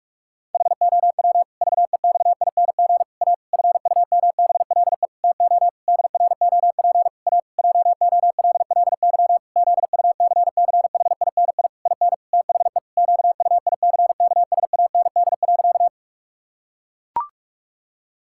Morse code, heard as a keyed tone, about 44 wpm, 700 Hz. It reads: HOW VEXING A FUMBLE TO DROP A JOLLY ZUCCHINI IN THE QUICKSAND.
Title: Pangram Challenge v2 - 45wpm